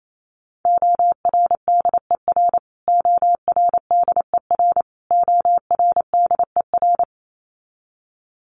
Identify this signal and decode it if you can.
Morse code, heard as a keyed tone, about 28 wpm, 700 Hz. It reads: ORDER ORDER ORDER
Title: Top 500 Words Repeated 28wpm